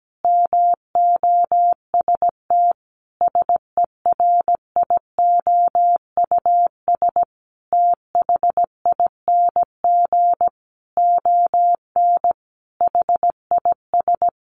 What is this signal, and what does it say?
Morse code, heard as a keyed tone, about 17 wpm, 700 Hz. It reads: MOST SERIOUS THING ON HIS
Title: Sentences from Top 1100 Words 17wpm